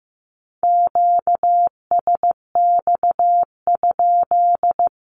Morse code, heard as a keyed tone, about 15 wpm, 700 Hz. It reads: QSX?
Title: Q-Codes 15wpm